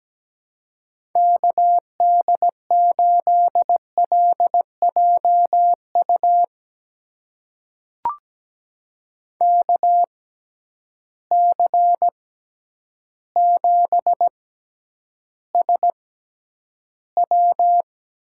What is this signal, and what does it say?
Morse code, heard as a keyed tone, about 17 wpm, spaced as for 9 wpm, 700 Hz. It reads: KD8LJU KC7SW
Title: USA Call Signs - 6x Character Spacing - 17wpm